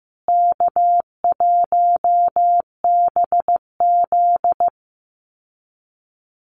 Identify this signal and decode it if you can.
Morse code, heard as a keyed tone, about 15 wpm, 700 Hz. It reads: K1BZ
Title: Call Signs Repeated 2x - 15wpm